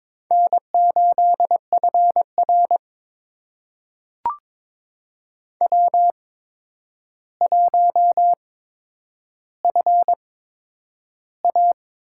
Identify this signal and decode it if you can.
Morse code, heard as a keyed tone, about 22 wpm, spaced as for 9 wpm, 700 Hz. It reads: N8FR W1FA
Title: USA Call Signs - 4 Characters - 8x Character Spacing - 22wpm